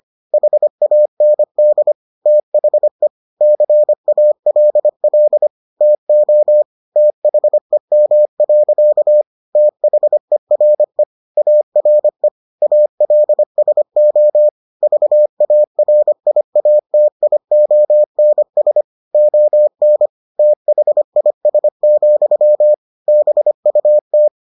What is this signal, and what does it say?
Morse code, heard as a keyed tone, about 25 wpm, 600 Hz. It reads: HAND THE CALL TO THEM. THERE ARE ALSO VARIATIONS ON THIS, BUT